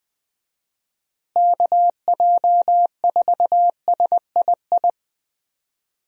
Morse code, heard as a keyed tone, about 20 wpm, 700 Hz. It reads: KJ4SII
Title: USA Call Signs - 6 Characters - Repeated 2x - 20wpm